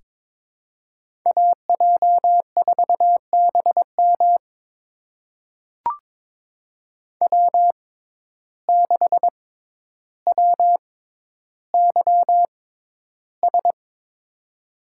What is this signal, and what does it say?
Morse code, heard as a keyed tone, about 22 wpm, spaced as for 11 wpm, 700 Hz. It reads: AJ4BM W6WYS
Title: USA Call Signs - 5 Characters - 6x Character Spacing - 22wpm